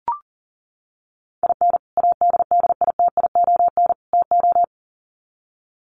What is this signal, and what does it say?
Morse code, heard as a keyed tone, about 40 wpm, 700 Hz. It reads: IN ADDITION TO
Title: Sets of 3 Words 40wpm